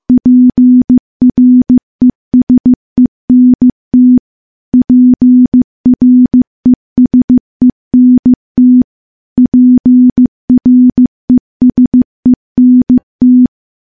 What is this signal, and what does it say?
Morse code, heard as a keyed tone, about 15 wpm, 250 Hz. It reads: PRESENT PRESENT PRESENT
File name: morse / morse-training-15WPM